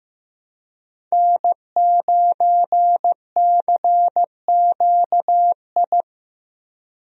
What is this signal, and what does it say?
Morse code, heard as a keyed tone, about 15 wpm, 700 Hz. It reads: N9CQI
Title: USA Call Signs - 5 Characters - Repeated 2x - 15wpm